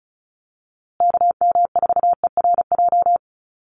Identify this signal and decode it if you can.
Morse code, heard as a keyed tone, about 34 wpm, 700 Hz. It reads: KM4ERJ
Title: USA Call Signs Repeated 2x 35wpm